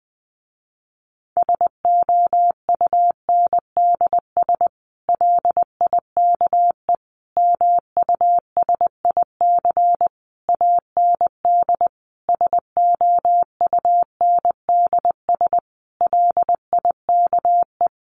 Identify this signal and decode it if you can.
Morse code, heard as a keyed tone, about 20 wpm, 700 Hz. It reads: SOUNDS LIKE MUSIC AND SOUNDS LIKE
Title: A Sound Of Thunder 20wpm